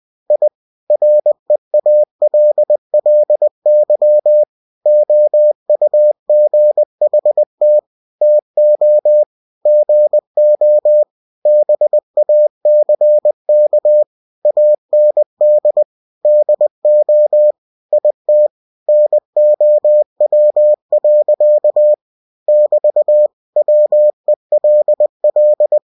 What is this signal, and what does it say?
Morse code, heard as a keyed tone, about 20 wpm, 600 Hz. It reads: I REALLY OUGHT TO GO BACK AND DO IT NOW. <BT> WELL
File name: cw-books / winnie-the-pooh / winnie-20wpm-page-0024